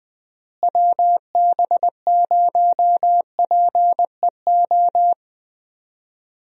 Morse code, heard as a keyed tone, about 20 wpm, 700 Hz. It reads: WB0PEO
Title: USA Call Signs - 6 Characters - Repeated 2x - 20wpm